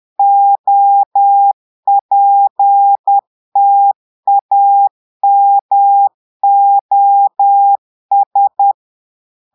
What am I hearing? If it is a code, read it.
Morse code, heard as a keyed tone, about 10 wpm, 800 Hz. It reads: OPTAMOS
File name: Portugais - texte en clair 1 - vitesse - 10 mots minute